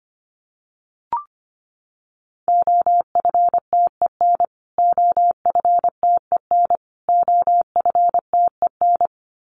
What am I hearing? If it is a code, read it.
Morse code, heard as a keyed tone, about 25 wpm, 700 Hz. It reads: OFTEN OFTEN OFTEN
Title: Top 300 Words Repeated 25wpm